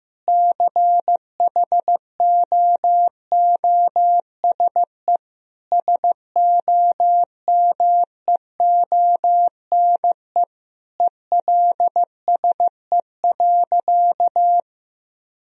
Morse code, heard as a keyed tone, about 16 wpm, 700 Hz. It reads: CHOOSE SOMEONE ELSE.
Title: Sentences from Top 1000 Words - Review Part 4 - 15wpm